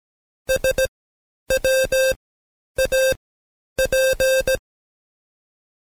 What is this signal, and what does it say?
Morse code, heard as a keyed tone, about 17 wpm, spaced as for 9 wpm, 530 Hz. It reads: SWAP
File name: Code Quick / AudioDownld2 / Seg131-Stepper4 9WPM